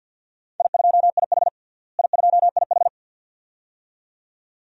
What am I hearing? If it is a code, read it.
Morse code, heard as a keyed tone, about 48 wpm, 700 Hz. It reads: I2IH I2IH
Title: DXCC-100 Call Signs - 4 Characters - Repeated 2x - 50wpm